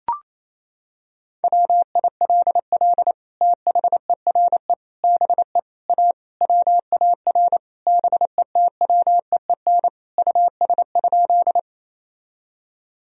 Morse code, heard as a keyed tone, about 28 wpm, 700 Hz. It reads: WILL THERE BE A WAR BETWEEN US?
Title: Sentences from Top 1000 Words - Review Part 2 - 28wpm